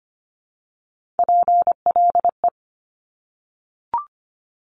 Morse code, heard as a keyed tone, about 25 wpm, 700 Hz. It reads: PLE
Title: Top Three Letter Combinations in Words 25wpm